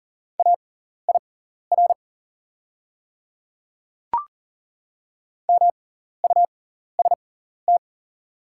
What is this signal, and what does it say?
Morse code, heard as a keyed tone, about 40 wpm, spaced as for 14 wpm, 690 Hz. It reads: AIR MUST
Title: 6x Character Spacing - Top 200 Words - 40wpm